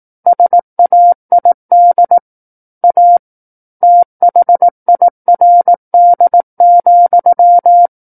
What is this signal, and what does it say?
Morse code, heard as a keyed tone, about 18 wpm, 700 Hz. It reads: SAID A THIRD,
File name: 18 WPM